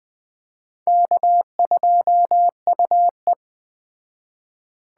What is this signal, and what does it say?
Morse code, heard as a keyed tone, about 20 wpm, 700 Hz. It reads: K2UE